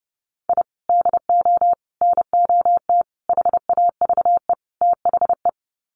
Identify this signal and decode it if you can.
Morse code, heard as a keyed tone, about 30 wpm, 700 Hz. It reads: I DO NOT HAVE THE
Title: Sets of 5 Words 30wpm